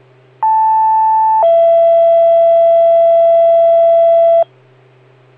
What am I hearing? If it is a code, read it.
Two-tone page, heard as a one-second tone, then a three-second tone, about 880 then 670 Hz. It is heard from 0.4 to 4.4 s.